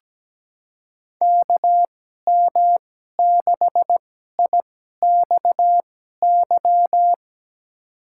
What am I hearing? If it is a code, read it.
Morse code, heard as a keyed tone, about 17 wpm, spaced as for 12 wpm, 700 Hz. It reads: KM6IXY
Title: USA Call Signs - 2x Character Spacing - 17wpm